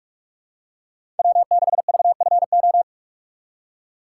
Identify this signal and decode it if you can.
Morse code, heard as a keyed tone, about 43 wpm, 700 Hz. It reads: W6VFO
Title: USA Call Signs - 5 Characters - 2x Character Spacing - 45wpm